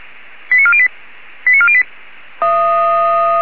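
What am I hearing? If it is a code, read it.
Five-tone selective call (ZVEI), heard as quick runs of five tones, tones of about 70 ms. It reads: 89408 89408
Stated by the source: Received on an SDR; ZVEI five-tone call.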